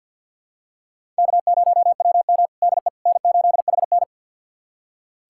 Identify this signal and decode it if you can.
Morse code, heard as a keyed tone, about 45 wpm, 700 Hz. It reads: K0WM DE N8HN
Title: USA Callsign DE Callsign - 4 Characters - 50wpm